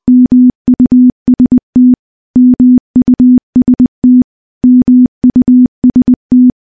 Morse code, heard as a keyed tone, about 20 wpm, 250 Hz. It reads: MUST MUST MUST
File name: morse-training-20WPM